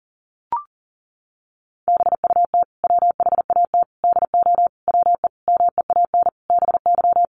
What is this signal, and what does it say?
Morse code, heard as a keyed tone, about 40 wpm, 700 Hz. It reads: BUT WHAT DO WE MEAN BY
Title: Sentences from Top 1100 Words 40wpm